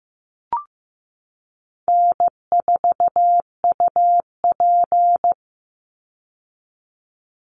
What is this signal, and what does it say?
Morse code, heard as a keyed tone, about 15 wpm, 700 Hz. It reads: N4UP
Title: Easy Call Signs Repeated 2x - 15wpm